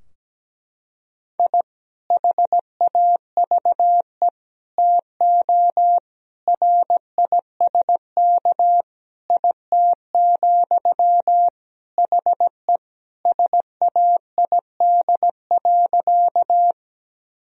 Morse code, heard as a keyed tone, about 17 wpm, 700 Hz. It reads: I HAVE TO RISK IT, HE SAID.
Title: Sentences from Top 1000 Words - Review Part 4 - 17wpm